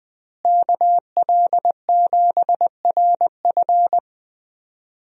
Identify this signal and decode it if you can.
Morse code, heard as a keyed tone, about 20 wpm, 700 Hz. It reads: KL7RF
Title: DXCC-100 Call Signs - 5 Characters - Repeated 2x - 20wpm